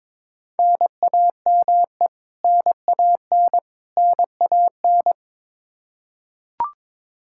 Morse code, heard as a keyed tone, about 22 wpm, 700 Hz. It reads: NAME NAN NAN
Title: Practice QSO - Exchange 1 - Beginning - 22wpm